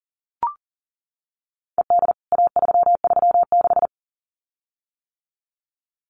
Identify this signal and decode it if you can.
Morse code, heard as a keyed tone, about 40 wpm, 700 Hz. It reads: ED A336
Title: CWT Exchange 40wpm